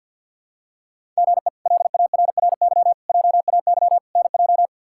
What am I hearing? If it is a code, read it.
Morse code, heard as a keyed tone, about 49 wpm, 700 Hz. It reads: GE LARRY JAY NJ
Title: K1USN SST QSO Practice - 50wpm